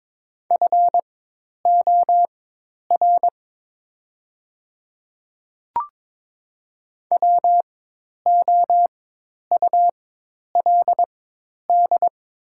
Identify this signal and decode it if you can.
Morse code, heard as a keyed tone, about 22 wpm, spaced as for 10 wpm, 700 Hz. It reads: FOR WOULD
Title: 4x Character Spacing - Top 100 Words - 22wpm